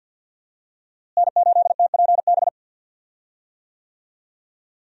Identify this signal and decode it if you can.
Morse code, heard as a keyed tone, about 46 wpm, 700 Hz. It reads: N8TPB
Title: USA Call Signs - 5 Characters - Repeated 3x - 50wpm